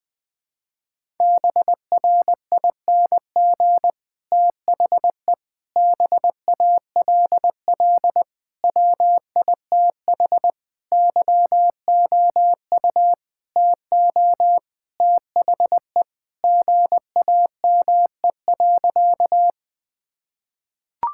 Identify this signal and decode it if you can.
Morse code, heard as a keyed tone, about 20 wpm, 700 Hz. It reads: BRING THE BALL WITH YOU TO THE GAME.
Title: Sentences from Top 1000 Words - Review Part 6 - 20wpm